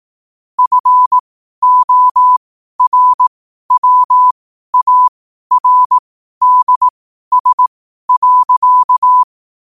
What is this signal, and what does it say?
Morse code, heard as a keyed tone, about 18 wpm, spaced as for 12 wpm, 1000 Hz. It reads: FORWARDS.
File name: qotdfiles / 12wpm